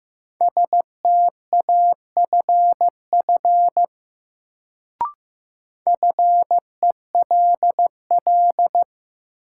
Morse code, heard as a keyed tone, about 15 wpm, 700 Hz. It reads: STAFF FELL
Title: Sets of 5 Words Encore - New Word Drill - 15wpm